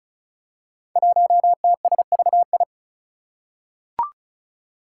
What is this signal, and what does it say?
Morse code, heard as a keyed tone, about 36 wpm, 700 Hz. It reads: 1TSVI
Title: Five Challenging Letter-Number Combinations 35wpm